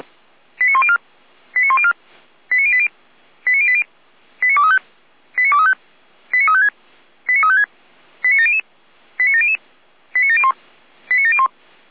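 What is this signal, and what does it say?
Five-tone selective call (ZVEI), heard as quick runs of five tones, tones of about 70 ms. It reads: 89104 89104 89080 89080 89235 89235 89356 89356 89700 89700 89701 89701
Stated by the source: ZVEI five-tone call received by SDR.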